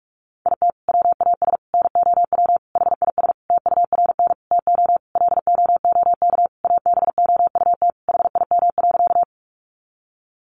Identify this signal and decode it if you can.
Morse code, heard as a keyed tone, about 44 wpm, 700 Hz. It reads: IT WAS NOW HIS TURN TO LOOK ABOUT HIM.
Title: Sentences from Top 200 Words 45wpm